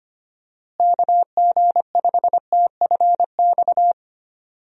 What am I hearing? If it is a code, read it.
Morse code, heard as a keyed tone, about 25 wpm, 700 Hz. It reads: KG5TFX
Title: USA Call Signs - 6 Characters - Repeated 2x - 25wpm